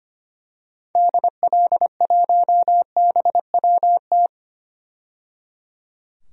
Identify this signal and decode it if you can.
Morse code, heard as a keyed tone, about 25 wpm, 690 Hz. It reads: DL1BWT
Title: DXCC-100 Call Signs - 6 Characters - 25wpm